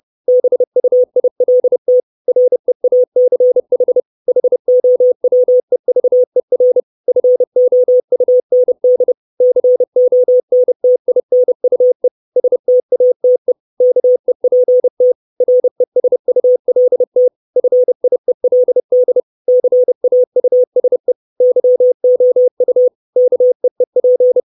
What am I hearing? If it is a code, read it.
Morse code, heard as a keyed tone, about 30 wpm, 500 Hz. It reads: BUILT REACH HOWEVER FOUND CONTINUE STATE KEPT RESULT FIELD CAUSE YOU KEEP
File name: 632 Words 30WPM